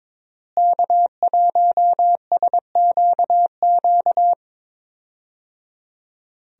Morse code, heard as a keyed tone, about 22 wpm, 700 Hz. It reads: K1SQQ
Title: USA Call Signs - 5 Characters - Repeated 3x - 22wpm